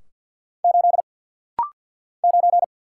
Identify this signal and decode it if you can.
Morse code, heard as a keyed tone, about 50 wpm, 690 Hz. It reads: 8 9
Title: Single Letter-Number - Rapid-Fire 50wpm